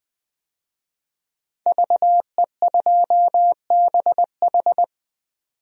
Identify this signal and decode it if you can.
Morse code, heard as a keyed tone, about 20 wpm, 700 Hz. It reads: VE2BH
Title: DXCC-100 Call Signs - 5 Characters - 20wpm